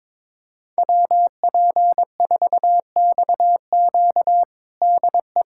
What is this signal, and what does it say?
Morse code, heard as a keyed tone, about 22 wpm, 700 Hz. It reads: WP4XQ DE
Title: USA Callsign DE Callsign 22wpm